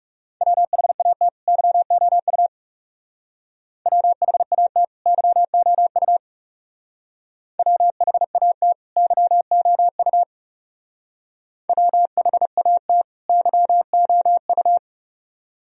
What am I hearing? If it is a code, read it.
Morse code, heard as a keyed tone, about 36 wpm, 700 Hz. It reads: WHAT YOU WHAT YOU WHAT YOU WHAT YOU
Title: Speed-Racing - Sets of 2 Words 30wpm